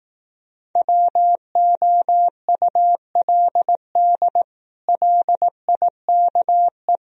Morse code, heard as a keyed tone, about 18 wpm, 700 Hz. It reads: WOULD LIKE
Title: Speed-Racing - Sets of 4 Words 15wpm